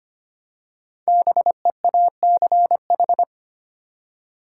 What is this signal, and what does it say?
Morse code, heard as a keyed tone, about 25 wpm, 700 Hz. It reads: BEACH
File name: 5 Letter Words - First 3 Letters Given - 25wpm